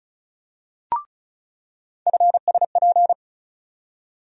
Morse code, heard as a keyed tone, about 36 wpm, 700 Hz. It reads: FSP
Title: USA Call Sign Postfixes - 3 Letters 35wpm